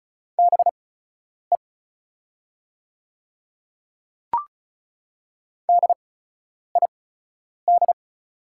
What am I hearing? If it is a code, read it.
Morse code, heard as a keyed tone, about 36 wpm, spaced as for 10 wpm, 700 Hz. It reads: BE DID
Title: 8x Character Spacing - Top 100 Words - 35wpm